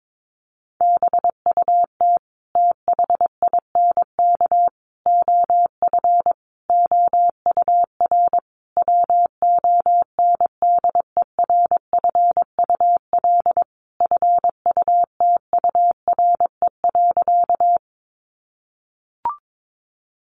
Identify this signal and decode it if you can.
Morse code, heard as a keyed tone, about 22 wpm, 700 Hz. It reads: BUT THINK OF OUR WONDERFUL FUTURE.
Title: Sentences from Top 1100 Words 22wpm